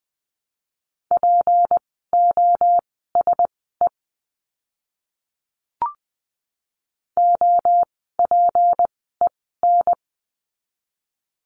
Morse code, heard as a keyed tone, about 20 wpm, spaced as for 14 wpm, 700 Hz. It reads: POSE OPEN
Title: 153 Lesson 25f - Letter p - Words 20wpm